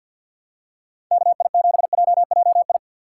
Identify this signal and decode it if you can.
Morse code, heard as a keyed tone, about 51 wpm, 700 Hz. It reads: KI7JJI
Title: USA Call Signs - 6 Characters - 3x Character Spacing - 50wpm